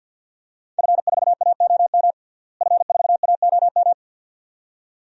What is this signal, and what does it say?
Morse code, heard as a keyed tone, about 51 wpm, 700 Hz. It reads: F4AOM F4AOM
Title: DXCC-100 Call Signs - 5 Characters - Repeated 2x - 50wpm